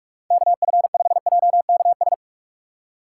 Morse code, heard as a keyed tone, about 46 wpm, 700 Hz. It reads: KF5JKS
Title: USA Call Signs - 2x Character Spacing - 45wpm